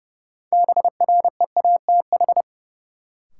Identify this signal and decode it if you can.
Morse code, heard as a keyed tone, about 30 wpm, 700 Hz. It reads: BREATH
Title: Sets of 5 Words Encore - New Word Drill - 30wpm